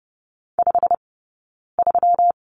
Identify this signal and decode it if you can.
Morse code, heard as a keyed tone, about 30 wpm, 700 Hz. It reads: 5 3
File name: Lesson 20d - Number 3 - Two Characters 30wpm